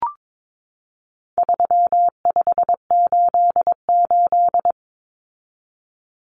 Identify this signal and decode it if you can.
Morse code, heard as a keyed tone, about 22 wpm, 700 Hz. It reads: 3588
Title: Common Cut Numbers - 4-Digit 22wpm